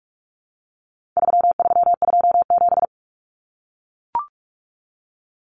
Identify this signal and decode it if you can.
Morse code, heard as a keyed tone, about 44 wpm, 700 Hz. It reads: 3327
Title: Common Cut Numbers - 4-Digit 45wpm